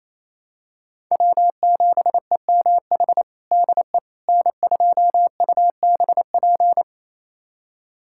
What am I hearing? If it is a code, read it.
Morse code, heard as a keyed tone, about 28 wpm, 700 Hz. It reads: W7EMH DE N2UBP
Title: USA Callsign DE Callsign - 5 Characters - 28wpm